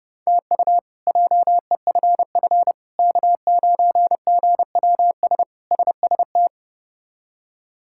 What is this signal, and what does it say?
Morse code, heard as a keyed tone, about 30 wpm, 700 Hz. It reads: TU JEFF K9GWS SST
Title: K1USN SST QSO Practice - 30wpm